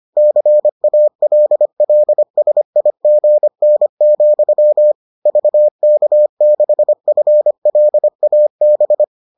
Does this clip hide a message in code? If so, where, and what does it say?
Yes: Morse code from 0.2 to 9.0 s, heard as a keyed tone, about 25 wpm, 600 Hz. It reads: CALLSIGN, VK6FLAB